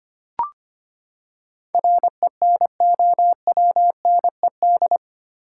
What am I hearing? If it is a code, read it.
Morse code, heard as a keyed tone, about 25 wpm, 700 Hz. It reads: RENOWNED
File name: Eight Letter Words - 25wpm